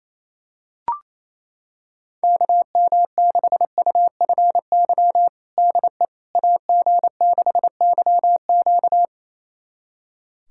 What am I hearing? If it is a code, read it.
Morse code, heard as a keyed tone, about 28 wpm, 700 Hz. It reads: KM6UFY DE AG6YQ
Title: USA Callsign DE Callsign 28wpm